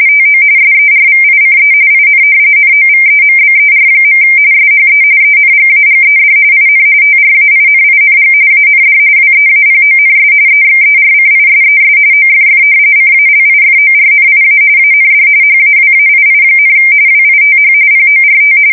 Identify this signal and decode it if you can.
RTTY, about 45.45 baud. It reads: TO SERVICE IN NOVEMBER. SHE ENGAGED IN TRAINING AND WEAPONS EXERCISES OFF CALIFORNIA AND HAWAII, WITH PORT
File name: fsk170 / battleshipnj